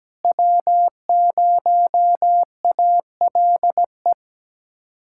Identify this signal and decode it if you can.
Morse code, heard as a keyed tone, about 17 wpm, 700 Hz. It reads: W0ALE
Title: USA Callsign DE Callsign - 5 Characters - 17wpm